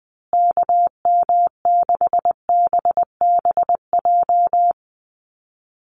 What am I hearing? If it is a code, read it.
Morse code, heard as a keyed tone, about 20 wpm, 700 Hz. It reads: KM6BBJ
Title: USA Call Signs Repeated 3x 20wpm